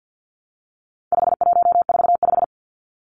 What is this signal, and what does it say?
Morse code, heard as a keyed tone, about 50 wpm, 700 Hz. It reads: 5145
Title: Numbers - 4-Digit 50wpm